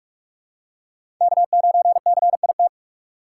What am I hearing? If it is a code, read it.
Morse code, heard as a keyed tone, about 44 wpm, 700 Hz. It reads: K9CIT
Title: USA Call Signs - 5 Characters - 6x Character Spacing - 45wpm